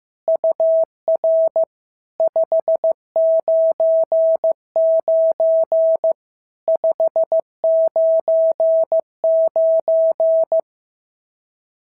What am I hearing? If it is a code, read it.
Morse code, heard as a keyed tone, about 14 wpm, 650 Hz. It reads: UR 599 599
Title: SOTA QSO Practice 15wpm